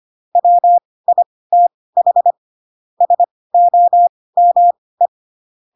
Morse code, heard as a keyed tone, about 25 wpm, spaced as for 18 wpm, 700 Hz. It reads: WITH SOME
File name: Pinocchio / Pinocchio ch7 12 25 ik1hge